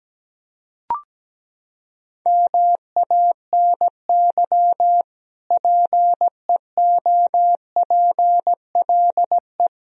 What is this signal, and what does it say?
Morse code, heard as a keyed tone, about 17 wpm, 700 Hz. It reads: MANY PEOPLE
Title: Sentences from Top 1000 Words - Review Part 1 - 17wpm